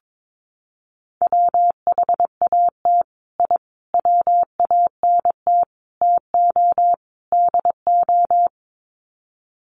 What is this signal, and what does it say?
Morse code, heard as a keyed tone, about 22 wpm, 700 Hz. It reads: WHAT I WANT TO DO
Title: Sets of 5 Words Encore 22wpm